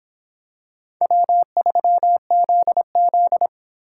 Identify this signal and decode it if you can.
Morse code, heard as a keyed tone, about 26 wpm, 700 Hz. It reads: W3ZZ